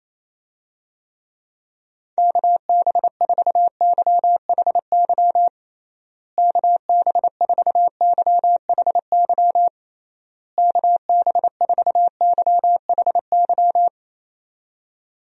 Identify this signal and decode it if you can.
Morse code, heard as a keyed tone, about 28 wpm, 700 Hz. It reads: KB4YHY KB4YHY KB4YHY
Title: USA Call Signs - 6 Characters - Repeated 3x - 28wpm